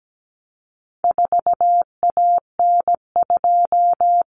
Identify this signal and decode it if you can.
Morse code, heard as a keyed tone, about 17 wpm, 700 Hz. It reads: 4AN2
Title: Common Cut Numbers - 4-Digit 17wpm